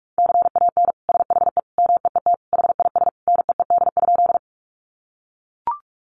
Morse code, heard as a keyed tone, about 44 wpm, 700 Hz. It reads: CAN SHE MEET HIS NEED?
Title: Sentences from Top 1100 Words 45wpm